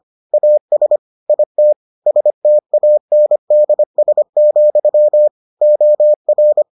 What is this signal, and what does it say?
Morse code, heard as a keyed tone, about 25 wpm, 600 Hz. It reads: AS IT STANDS, OR